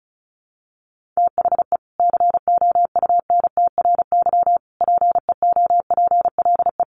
Morse code, heard as a keyed tone, about 34 wpm, 700 Hz. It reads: THE COUNTRY PEOPLE
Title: Sentences from Top 1100 Words 35wpm